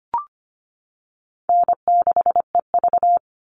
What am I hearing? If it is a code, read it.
Morse code, heard as a keyed tone, about 25 wpm, 700 Hz. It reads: N6EV
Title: Call Signs - 25wpm